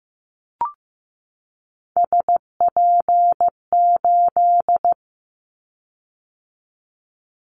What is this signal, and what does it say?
Morse code, heard as a keyed tone, about 15 wpm, 700 Hz. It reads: SP8
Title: Callsign Prefixes for Top 25 DXCC - 15wpm